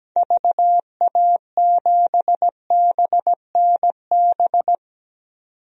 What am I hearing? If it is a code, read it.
Morse code, heard as a keyed tone, about 17 wpm, 700 Hz. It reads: VA7BNB
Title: DXCC-100 Call Signs - 6 Characters - 17wpm